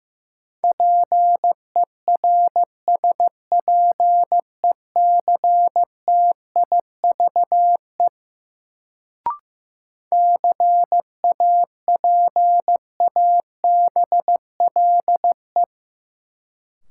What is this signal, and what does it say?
Morse code, heard as a keyed tone, about 15 wpm, 700 Hz. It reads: PERSPECTIVE CAPABLE
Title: Sentences from Top 1200 Words - New Word Drill - 15wpm